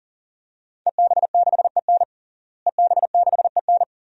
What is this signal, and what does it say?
Morse code, heard as a keyed tone, about 37 wpm, 700 Hz. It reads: EB6EN EB6EN
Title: DXCC-100 Call Signs - 5 Characters - Repeated 2x - 40wpm